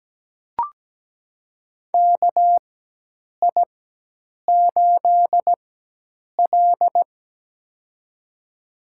Morse code, heard as a keyed tone, about 17 wpm, spaced as for 8 wpm, 700 Hz. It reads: KI8L
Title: USA Call Signs - 4 Characters - 4x Character Spacing - 17wpm